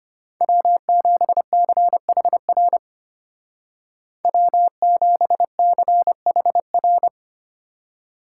Morse code, heard as a keyed tone, about 32 wpm, 700 Hz. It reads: W7CHR W7CHR